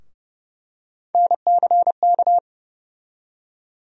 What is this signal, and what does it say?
Morse code, heard as a keyed tone, about 30 wpm, 700 Hz. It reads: NCK
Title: USA Call Sign Postfixes - 3 Letters 30wpm